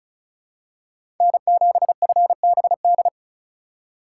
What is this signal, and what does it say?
Morse code, heard as a keyed tone, about 36 wpm, 700 Hz. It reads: N7FBD
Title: USA Call Signs - 6x Character Spacing - 35wpm